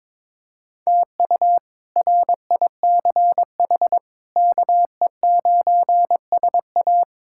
Morse code, heard as a keyed tone, about 22 wpm, 700 Hz. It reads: TU RICH KE9SA
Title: K1USN SST QSO Practice - 22wpm